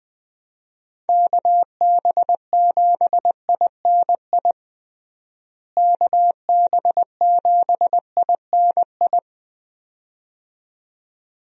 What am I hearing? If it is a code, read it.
Morse code, heard as a keyed tone, about 20 wpm, 700 Hz. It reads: KB7INI KB7INI
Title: USA Call Signs - 6 Characters - Repeated 2x - 20wpm